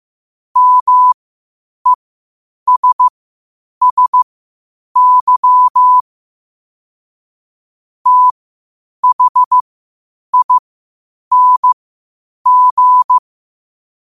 Morse code, heard as a keyed tone, about 15 wpm, spaced as for 8 wpm, 1000 Hz. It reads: MESSY THING
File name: qotdfiles / 8wpm